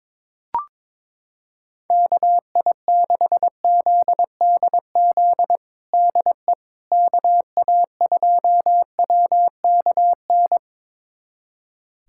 Morse code, heard as a keyed tone, about 22 wpm, 700 Hz. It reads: KI6ZDZ DE KA2WKN
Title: USA Callsign DE Callsign 22wpm